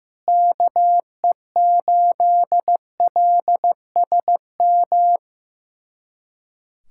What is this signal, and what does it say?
Morse code, heard as a keyed tone, about 15 wpm, 700 Hz. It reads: KE8LSM
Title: USA Callsign DE Callsign - 6 Characters - 15wpm